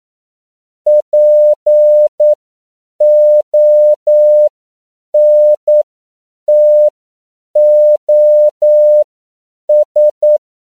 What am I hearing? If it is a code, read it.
Morse code, heard as a keyed tone, about 9 wpm, spaced as for 7 wpm, 600 Hz. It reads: PONTOS
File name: CW / 10PPM